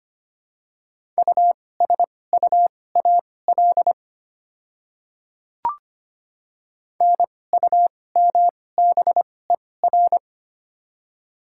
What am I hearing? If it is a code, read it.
Morse code, heard as a keyed tone, about 25 wpm, spaced as for 18 wpm, 690 Hz. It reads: USUAL NUMBER
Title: 2x Character Spacing - Top 500 Words - 25wpm